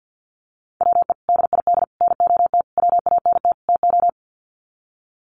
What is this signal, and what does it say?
Morse code, heard as a keyed tone, about 50 wpm, 700 Hz. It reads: WE DID NOT WANT TO
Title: Sets of 5 Words Encore 50wpm